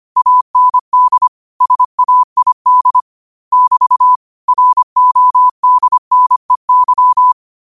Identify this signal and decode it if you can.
Morse code, heard as a keyed tone, about 25 wpm, 1000 Hz. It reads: AND SAID <BT> RODNEY
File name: qotdfiles / 25wpm